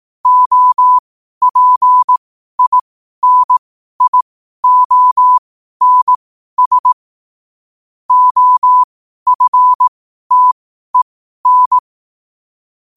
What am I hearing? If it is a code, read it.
Morse code, heard as a keyed tone, about 18 wpm, spaced as for 12 wpm, 1000 Hz. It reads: OPINIONS OFTEN
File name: 12 wpm